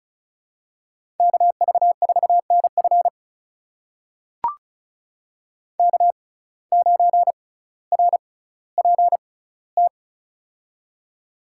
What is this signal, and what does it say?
Morse code, heard as a keyed tone, about 36 wpm, spaced as for 18 wpm, 700 Hz. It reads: KV4NF K9RPT
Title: USA Call Signs - 5 Characters - 6x Character Spacing - 35wpm